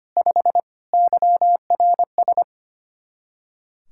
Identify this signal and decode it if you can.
Morse code, heard as a keyed tone, about 25 wpm, 700 Hz. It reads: 5 YRS
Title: K1USN SST QSO Practice 25wpm